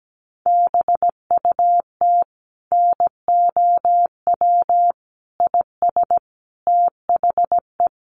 Morse code, heard as a keyed tone, about 17 wpm, 700 Hz. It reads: BUT NOW IS THE
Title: Sentences from Top 1500 Words 17wpm